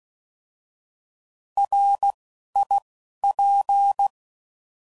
Morse code, heard as a keyed tone, about 16 wpm, 800 Hz. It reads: R I P
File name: Prefixes / By First Letter / Rs